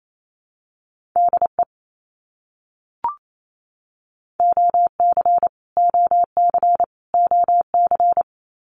Morse code, heard as a keyed tone, about 28 wpm, 700 Hz. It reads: DE OC OC OC
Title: Abbreviations Repeated 3x - 28wpm